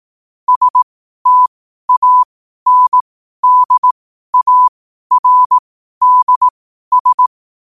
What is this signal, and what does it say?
Morse code, heard as a keyed tone, about 18 wpm, spaced as for 12 wpm, 1000 Hz. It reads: STANDARDS
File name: qotdfiles / 12wpm